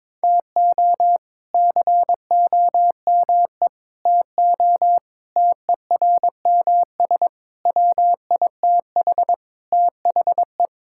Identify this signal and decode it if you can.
Morse code, heard as a keyed tone, about 22 wpm, 700 Hz. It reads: TO COME TO TERMS WITH THE
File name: practice / 22wpm / Sets of 8 Words - 22wpm